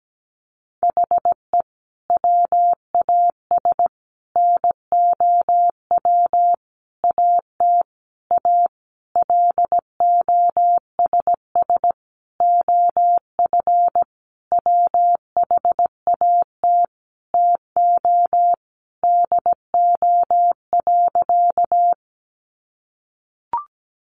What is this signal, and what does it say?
Morse code, heard as a keyed tone, about 17 wpm, 700 Hz. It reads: HE WAS NOW AT A LOSS OF WHAT TO DO.
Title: Sentences from Top 1400 Words 17wpm